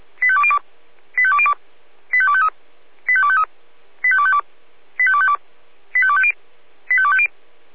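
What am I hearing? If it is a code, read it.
Five-tone selective call (ZVEI), heard as quick runs of five tones, tones of about 70 ms. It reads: 85202 85202 85253 85253 85252 85252 85250 85250
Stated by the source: ZVEI five-tone selective call 85102 received by SDR.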